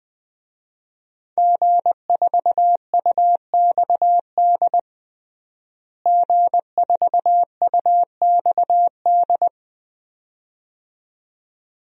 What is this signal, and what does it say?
Morse code, heard as a keyed tone, about 20 wpm, 700 Hz. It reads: G4UXD G4UXD
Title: DXCC-100 Call Signs - 5 Characters - Repeated 2x - 20wpm